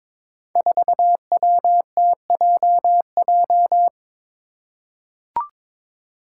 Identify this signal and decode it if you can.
Morse code, heard as a keyed tone, about 22 wpm, 700 Hz. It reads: 4WTJJ
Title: Five Challenging Letter-Number Combinations 22wpm